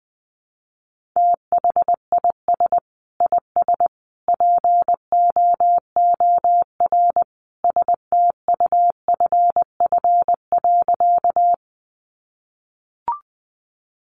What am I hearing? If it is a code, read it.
Morse code, heard as a keyed tone, about 20 wpm, 700 Hz. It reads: THIS IS POOR STUFF.
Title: Sentences from Top 1400 Words 20wpm